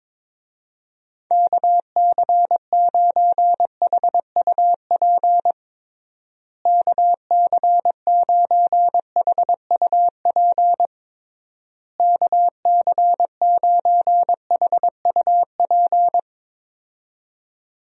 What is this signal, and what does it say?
Morse code, heard as a keyed tone, about 22 wpm, 700 Hz. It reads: KC9HUP KC9HUP KC9HUP
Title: USA Call Signs - 6 Characters - Repeated 3x - 22wpm